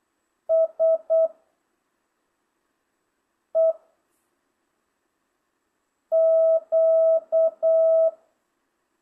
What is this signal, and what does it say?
Morse code, heard as a keyed tone, about 8 wpm, 660 Hz. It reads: S E Q